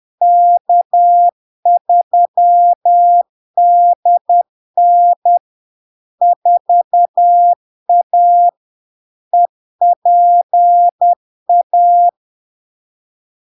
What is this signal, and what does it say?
Morse code, heard as a keyed tone, about 10 wpm, 700 Hz. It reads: K3DN 4A EPA
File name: LARK-FD-2022-Log-in-CW-10wpm-700-Hz